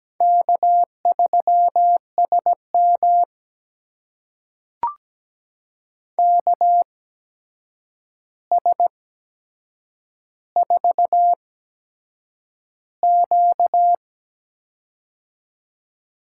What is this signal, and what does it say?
Morse code, heard as a keyed tone, about 17 wpm, spaced as for 7 wpm, 700 Hz. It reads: K3SM KS4Q
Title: USA Call Signs - 4 Characters - 8x Character Spacing - 17wpm